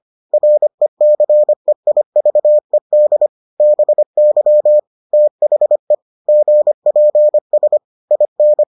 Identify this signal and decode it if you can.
Morse code, heard as a keyed tone, about 25 wpm, 600 Hz. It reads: RECEIVED BY THE GPS IN